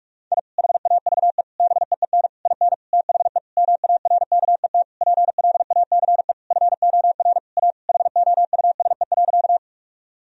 Morse code, heard as a keyed tone, about 46 wpm, 700 Hz. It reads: I HAVE BEEN IN THE MARKET PLACE FOR A HOUSE.
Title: Sentences from Top 1000 Words - Review Part 3 - 45wpm